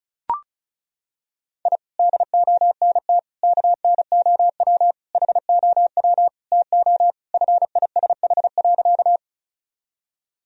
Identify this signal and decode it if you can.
Morse code, heard as a keyed tone, about 36 wpm, 700 Hz. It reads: I DONT KNOW HOW TO FISH.
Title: Sentences from Top 1000 Words - Review Part 1 - 35wpm